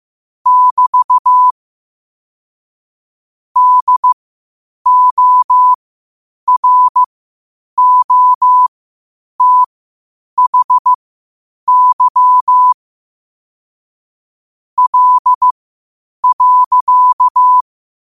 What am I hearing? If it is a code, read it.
Morse code, heard as a keyed tone, about 15 wpm, spaced as for 8 wpm, 1000 Hz. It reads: <BT> DOROTHY L.
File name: qotdfiles / 8wpm